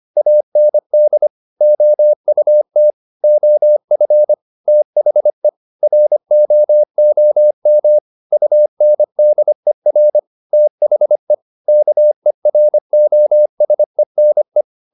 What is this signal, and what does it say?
Morse code, heard as a keyed tone, about 25 wpm, 600 Hz. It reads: AND OUT OF THE ROOM UNDER THE KEROSENE